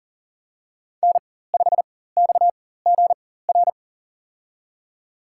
Morse code, heard as a keyed tone, about 40 wpm, spaced as for 19 wpm, 700 Hz. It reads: N5XGR
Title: USA Call Signs - 5 Characters - 4x Character Spacing - 40wpm